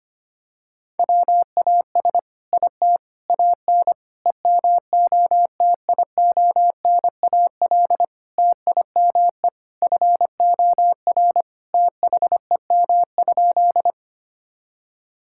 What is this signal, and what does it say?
Morse code, heard as a keyed tone, about 25 wpm, 700 Hz. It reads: WAS IT AN EMOTIONAL TIME FOR THEM?
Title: Sentences from Top 1000 Words - Review Part 5 - 25wpm